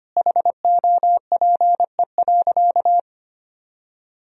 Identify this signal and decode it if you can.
Morse code, heard as a keyed tone, about 25 wpm, 700 Hz. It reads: HOPE.
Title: Sentences from Top 1000 Words - Review Part 1 - 25wpm